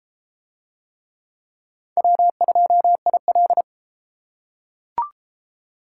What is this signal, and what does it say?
Morse code, heard as a keyed tone, about 32 wpm, 700 Hz. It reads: W2IL